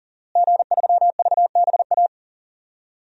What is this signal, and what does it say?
Morse code, heard as a keyed tone, about 40 wpm, 700 Hz. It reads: G3VBA
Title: DXCC-100 Call Signs - 5 Characters - Repeated 3x - 40wpm